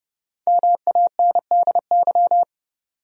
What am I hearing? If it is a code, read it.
Morse code, heard as a keyed tone, about 30 wpm, 700 Hz. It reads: MANDY
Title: USA Common Names - Max 5 Letters - 30wpm